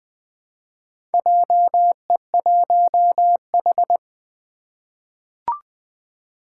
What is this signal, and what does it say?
Morse code, heard as a keyed tone, about 20 wpm, 700 Hz. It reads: JE1H
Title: Four Challenging Letter-Number Combinations 20wpm